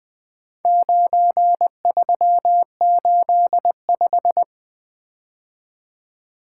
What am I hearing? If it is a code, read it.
Morse code, heard as a keyed tone, about 20 wpm, 700 Hz. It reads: 9385
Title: Four Letter-Number Combinations - 20wpm